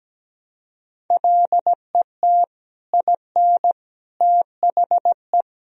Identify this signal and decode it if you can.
Morse code, heard as a keyed tone, about 17 wpm, 700 Hz. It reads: LET IN THE
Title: Sentences from Top 1000 Words - Review Part 1 - 17wpm